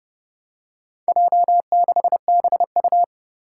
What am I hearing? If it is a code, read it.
Morse code, heard as a keyed tone, about 30 wpm, 700 Hz. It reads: J6BU
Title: Four Challenging Letter-Number Combinations 30wpm